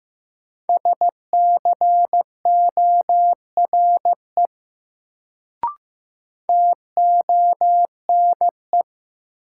Morse code, heard as a keyed tone, about 15 wpm, 700 Hz. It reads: SCORE TONE
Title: Sentences from Top 1600 Words - New Word Drill - 15wpm